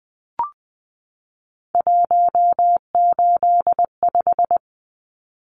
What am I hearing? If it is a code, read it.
Morse code, heard as a keyed tone, about 20 wpm, 700 Hz. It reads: 185
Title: Numbers - 3-Digit 20wpm